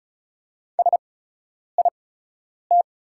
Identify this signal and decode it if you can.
Morse code, heard as a keyed tone, about 36 wpm, spaced as for 10 wpm, 700 Hz. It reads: SIT
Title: 8x Character Spacing - Top 500 Words - 35wpm